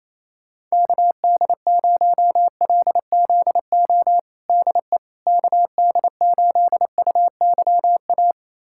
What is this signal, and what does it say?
Morse code, heard as a keyed tone, about 27 wpm, 700 Hz. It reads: KD0LZO DE KD8UYA
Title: USA Callsign DE Callsign - 6 Characters - 28wpm